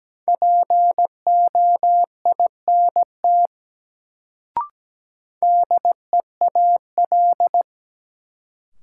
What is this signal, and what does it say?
Morse code, heard as a keyed tone, about 17 wpm, 700 Hz. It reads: POINT DEAL
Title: Sets of 3 Words - New Word Drill - 17wpm